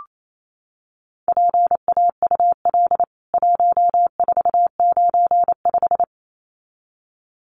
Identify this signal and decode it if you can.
Morse code, heard as a keyed tone, about 28 wpm, 700 Hz. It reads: PAUL 1495
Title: CWT Exchange 28wpm